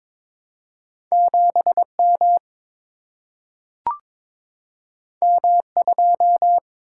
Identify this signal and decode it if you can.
Morse code, heard as a keyed tone, about 22 wpm, 700 Hz. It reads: 7 M M 2
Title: Single-Letter Single-Number 22wpm